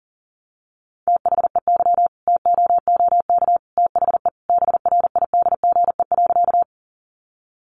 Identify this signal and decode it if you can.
Morse code, heard as a keyed tone, about 40 wpm, 700 Hz. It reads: THEY TOOK THE BRIDGE.
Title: Sentences from Top 1200 Words 40wpm